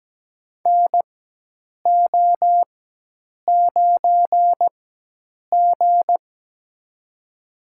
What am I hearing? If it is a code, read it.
Morse code, heard as a keyed tone, about 17 wpm, spaced as for 8 wpm, 700 Hz. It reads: NO9G
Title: USA Call Signs - 4x Character Spacing - 17wpm